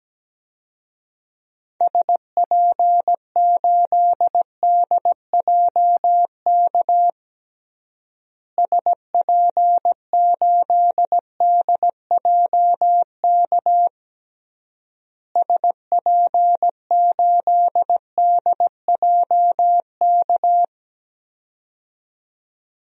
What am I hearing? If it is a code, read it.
Morse code, heard as a keyed tone, about 17 wpm, 700 Hz. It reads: SP8DJK SP8DJK SP8DJK
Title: DXCC-100 Call Signs - 6 Characters - Repeated 3x - 17wpm